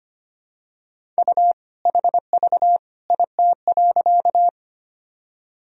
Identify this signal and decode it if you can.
Morse code, heard as a keyed tone, about 25 wpm, 700 Hz. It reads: U HV IT.
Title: Sentences from Top 100 Words and Abbreviations 25wpm